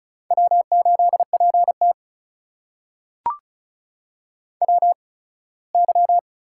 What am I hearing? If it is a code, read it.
Morse code, heard as a keyed tone, about 36 wpm, spaced as for 21 wpm, 700 Hz. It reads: W8PT WY
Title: USA Call Signs - 4 Characters - 8x Character Spacing - 35wpm